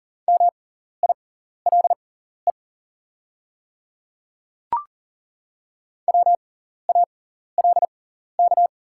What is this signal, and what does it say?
Morse code, heard as a keyed tone, about 40 wpm, spaced as for 14 wpm, 690 Hz. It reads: MILE WALK
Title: 6x Character Spacing - Top 300 Words - 40wpm